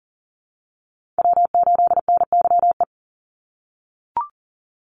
Morse code, heard as a keyed tone, about 39 wpm, 700 Hz. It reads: W8NYE
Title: USA Call Signs 40wpm